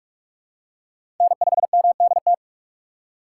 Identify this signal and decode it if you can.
Morse code, heard as a keyed tone, about 44 wpm, 700 Hz. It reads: N5MDT
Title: USA Call Signs - 5 Characters - 8x Character Spacing - 45wpm